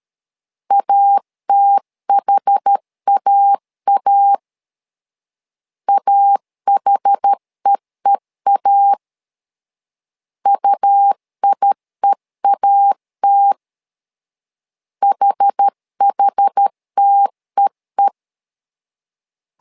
Morse code, heard as a keyed tone, about 13 wpm, 800 Hz. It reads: ATHAA AHEEA UIEAT HHTEE
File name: LEZIONI / 2 UHDN / 2 UH 12wpm 1439